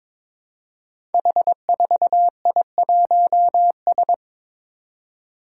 Morse code, heard as a keyed tone, about 22 wpm, 700 Hz. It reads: H4I1S
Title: Five Challenging Letter-Number Combinations 22wpm